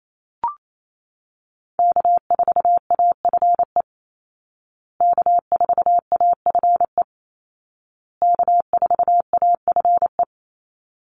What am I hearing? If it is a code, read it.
Morse code, heard as a keyed tone, about 28 wpm, 700 Hz. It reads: K4AFE K4AFE K4AFE
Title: Call Signs Repeated 2x - 28wpm